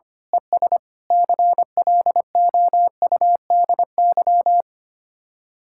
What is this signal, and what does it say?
Morse code, heard as a keyed tone, about 24 wpm, 700 Hz. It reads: ES CLOUDY
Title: Speed-Racing Common QSO Phrases Encore - 25wpm